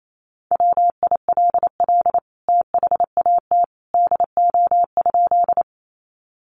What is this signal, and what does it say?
Morse code, heard as a keyed tone, about 28 wpm, 700 Hz. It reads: WILL THAT DO?
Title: Sentences from Top 100 Words 28wpm